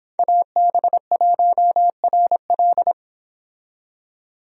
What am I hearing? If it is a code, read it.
Morse code, heard as a keyed tone, about 30 wpm, 700 Hz. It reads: AB1RL